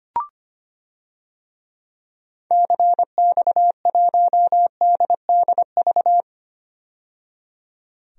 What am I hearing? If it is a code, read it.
Morse code, heard as a keyed tone, about 25 wpm, 700 Hz. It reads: CX1DDV
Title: DXCC-100 Call Signs - 6 Characters - 25wpm